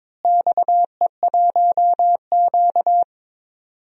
Morse code, heard as a keyed tone, about 22 wpm, 700 Hz. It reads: XE1Q
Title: DXCC-100 Call Signs - 4 Characters - Repeated 2x - 22wpm